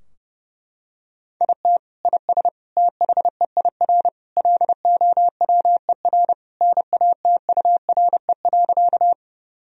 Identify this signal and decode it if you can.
Morse code, heard as a keyed tone, about 30 wpm, 700 Hz. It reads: IT IS THEIR LOWER NATURE.
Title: Sentences from Top 1000 Words - Review Part 4 - 30wpm